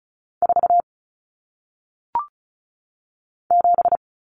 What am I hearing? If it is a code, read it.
Morse code, heard as a keyed tone, about 36 wpm, 700 Hz. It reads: 4 7
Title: Numbers - 1-Digit 35wpm